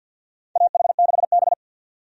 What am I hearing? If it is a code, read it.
Morse code, heard as a keyed tone, about 50 wpm, 700 Hz. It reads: AH6B
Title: USA Call Signs - 4 Characters - 50wpm